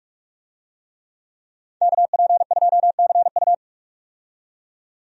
Morse code, heard as a keyed tone, about 46 wpm, 700 Hz. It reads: KP2CU
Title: DXCC-100 Call Signs - 5 Characters - 45wpm